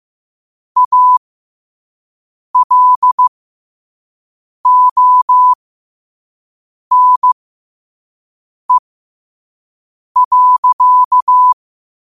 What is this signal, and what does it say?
Morse code, heard as a keyed tone, about 15 wpm, spaced as for 5 wpm, 1000 Hz. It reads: ALONE.
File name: qotdfiles / 5wpm